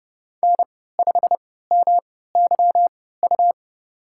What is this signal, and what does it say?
Morse code, heard as a keyed tone, about 30 wpm, spaced as for 17 wpm, 700 Hz. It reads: N5MYU
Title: USA Call Signs - 5 Characters - 3x Character Spacing - 30wpm